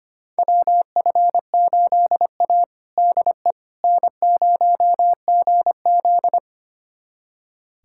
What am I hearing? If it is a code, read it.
Morse code, heard as a keyed tone, about 25 wpm, 700 Hz. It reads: WF8A DE N0GZ
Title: USA Callsign DE Callsign - 4 Characters - 25wpm